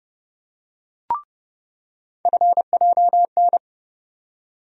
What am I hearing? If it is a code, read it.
Morse code, heard as a keyed tone, about 30 wpm, 700 Hz. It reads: FJN
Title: USA Call Sign Postfixes - 3 Letters 30wpm